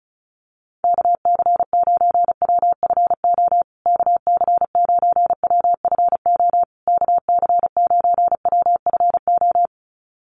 Morse code, heard as a keyed tone, about 34 wpm, 700 Hz. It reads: KC9WFO KC9WFO KC9WFO
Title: USA Call Signs Repeated 3x 35wpm